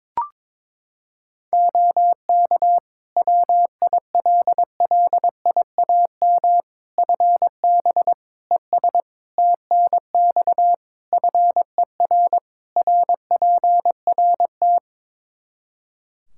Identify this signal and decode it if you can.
Morse code, heard as a keyed tone, about 22 wpm, 700 Hz. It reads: OK WILLIAM FB ES TNX FER RPRT
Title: Practice QSO - Exchange 2 - Info - 22wpm